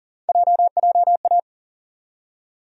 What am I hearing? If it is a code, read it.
Morse code, heard as a keyed tone, about 39 wpm, 700 Hz. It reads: JJA
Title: USA Call Sign Postfixes - 3 Letters 40wpm